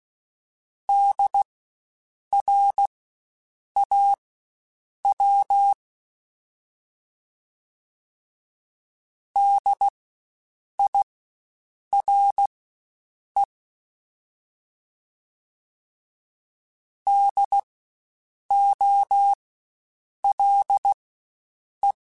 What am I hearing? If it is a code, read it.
Morse code, heard as a keyed tone, about 16 wpm, spaced as for 7 wpm, 800 Hz. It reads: DRAW DIRE DOLE
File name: Spaced Alphas / Spaced Alphas mp3s / Ds sp @15